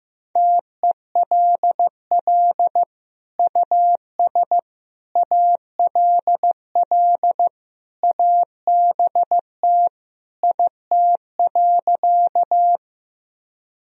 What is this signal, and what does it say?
Morse code, heard as a keyed tone, about 15 wpm, 700 Hz. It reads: TELL US ALL ABT IT.
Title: Sentences from Top 200 Words and Abbreviations 15wpm